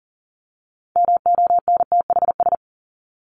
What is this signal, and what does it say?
Morse code, heard as a keyed tone, about 40 wpm, 700 Hz. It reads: MONTHS
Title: ES:S Suffix - Top 100 Words with Base Words 40wpm